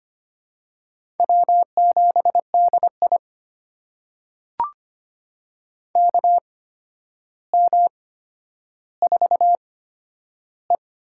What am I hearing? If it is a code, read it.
Morse code, heard as a keyed tone, about 25 wpm, spaced as for 11 wpm, 700 Hz. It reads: W7DI KM4E
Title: USA Call Signs - 4 Characters - 8x Character Spacing - 25wpm